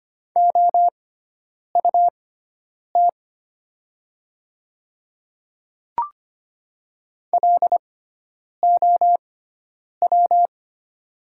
Morse code, heard as a keyed tone, about 25 wpm, spaced as for 9 wpm, 700 Hz. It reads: OUT LOW
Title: 6x Character Spacing - Top 200 Words - 25wpm